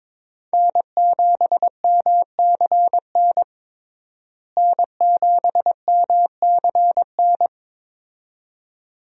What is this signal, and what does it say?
Morse code, heard as a keyed tone, about 22 wpm, 700 Hz. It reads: N7MCN N7MCN
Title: USA Call Signs - 5 Characters - Repeated 2x - 22wpm